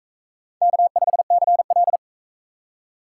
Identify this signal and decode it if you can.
Morse code, heard as a keyed tone, about 41 wpm, 700 Hz. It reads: K5CL